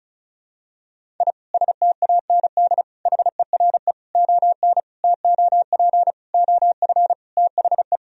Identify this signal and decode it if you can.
Morse code, heard as a keyed tone, about 35 wpm, 700 Hz. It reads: I STAND HERE ON TOP OF THE
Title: Sentences from Top 1000 Words - Review Part 1 - 35wpm